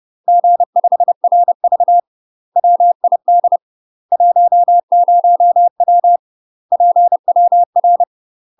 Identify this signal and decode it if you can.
Morse code, heard as a keyed tone, about 30 wpm, 700 Hz. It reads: G5RV WID 10W PWR
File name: UK MASTER QSOs TRAINING / 30WPM / UK MASTER QSOs 0033